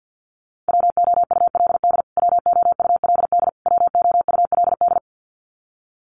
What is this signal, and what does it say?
Morse code, heard as a keyed tone, about 50 wpm, 700 Hz. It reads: WOULD WOULD WOULD
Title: Top 500 Words Repeated 50wpm